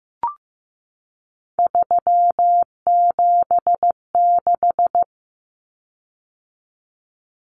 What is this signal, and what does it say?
Morse code, heard as a keyed tone, about 15 wpm, 700 Hz. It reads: 376
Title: Common Cut Numbers - 3-Digit 15wpm